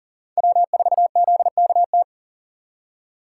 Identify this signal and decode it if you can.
Morse code, heard as a keyed tone, about 40 wpm, 700 Hz. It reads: W4ZKT
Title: USA Call Signs - 5 Characters - 3x Character Spacing - 40wpm